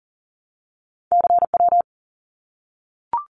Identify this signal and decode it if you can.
Morse code, heard as a keyed tone, about 40 wpm, 700 Hz. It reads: CW
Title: Abbreviations Repeated 3x - 40wpm